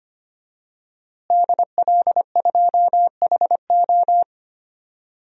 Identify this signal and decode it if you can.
Morse code, heard as a keyed tone, about 25 wpm, 700 Hz. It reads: DL2HO
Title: DXCC-100 Call Signs - 5 Characters - Repeated 2x - 25wpm